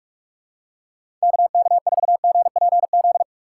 Morse code, heard as a keyed tone, about 41 wpm, 700 Hz. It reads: KK4GPZ
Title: USA Call Signs - 6 Characters - 6x Character Spacing - 45wpm